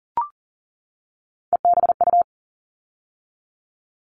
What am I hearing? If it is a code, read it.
Morse code, heard as a keyed tone, about 40 wpm, 700 Hz. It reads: EBU
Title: All Cut Numbers - 3-Digit 40wpm